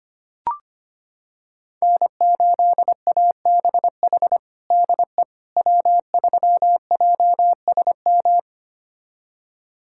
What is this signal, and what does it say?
Morse code, heard as a keyed tone, about 25 wpm, 700 Hz. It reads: N8ABH DE W3JSM
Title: USA Callsign DE Callsign - 5 Characters - 25wpm